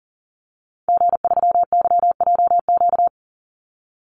Morse code, heard as a keyed tone, about 39 wpm, 700 Hz. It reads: G3YJQ
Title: Call Signs Repeated 3x - 40wpm